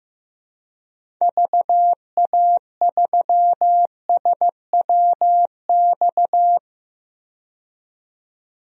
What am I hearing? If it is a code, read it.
Morse code, heard as a keyed tone, about 15 wpm, 700 Hz. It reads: VA3SWX
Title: DXCC-100 Call Signs - 6 Characters - 15wpm